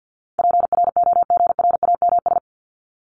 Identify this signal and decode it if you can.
Morse code, heard as a keyed tone, about 50 wpm, 700 Hz. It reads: PROGRAMS
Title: ES:S Suffix - Top 100 Words with Base Words 50wpm